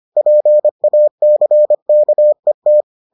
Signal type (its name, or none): Morse code